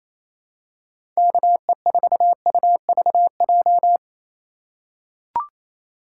Morse code, heard as a keyed tone, about 28 wpm, 700 Hz. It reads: KE4UVJ
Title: USA Call Signs - 4x Character Spacing - 28wpm